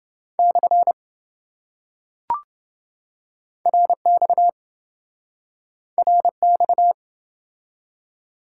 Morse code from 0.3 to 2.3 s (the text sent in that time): /